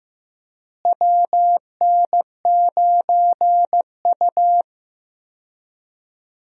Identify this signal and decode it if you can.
Morse code, heard as a keyed tone, about 16 wpm, 700 Hz. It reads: WN9U
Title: USA Call Signs - 4 Characters - Repeated 2x - 15wpm